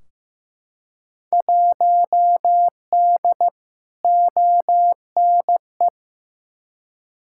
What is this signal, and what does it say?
Morse code, heard as a keyed tone, about 15 wpm, 700 Hz. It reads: 1D ONE
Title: Field Day Exchanges - 15wpm